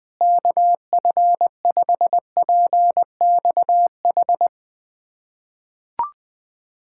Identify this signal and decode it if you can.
Morse code, heard as a keyed tone, about 21 wpm, 700 Hz. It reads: KF5PXH
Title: USA Callsign DE Callsign - 6 Characters - 20wpm